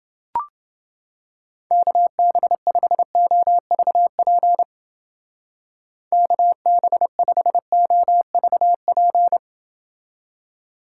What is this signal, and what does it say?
Morse code, heard as a keyed tone, about 29 wpm, 700 Hz. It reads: KB5OVP KB5OVP